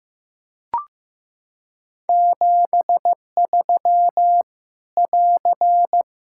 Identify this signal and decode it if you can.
Morse code, heard as a keyed tone, about 14 wpm, 700 Hz. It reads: 73 <AR>
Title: Practice QSO - Exchange 4 - Ending - 15wpm